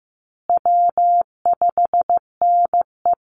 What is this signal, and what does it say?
Morse code, heard as a keyed tone, about 15 wpm, 700 Hz. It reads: W5NE
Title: Easy Call Signs Repeated 3x - 15wpm